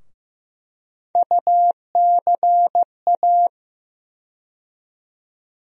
Morse code, heard as a keyed tone, about 15 wpm, 700 Hz. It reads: UCA
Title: USA Call Sign Postfixes - 3 Letters 15wpm